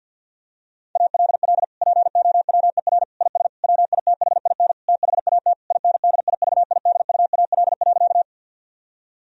Morse code, heard as a keyed tone, about 48 wpm, 700 Hz. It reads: ALL POWER IS WITHIN THAT INDIVIDUAL.
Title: Sentences from Top 1000 Words - Review Part 4 - 50wpm